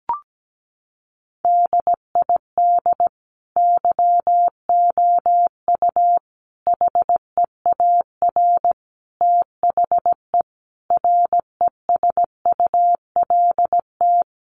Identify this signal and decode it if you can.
Morse code, heard as a keyed tone, about 17 wpm, 700 Hz. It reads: DID YOU HEAR THE RESULT
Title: Sentences from Top 1400 Words 17wpm